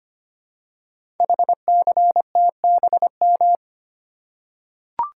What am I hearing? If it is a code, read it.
Morse code, heard as a keyed tone, about 25 wpm, 700 Hz. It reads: HCTBM
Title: Code Groups - 5 Letters - 25wpm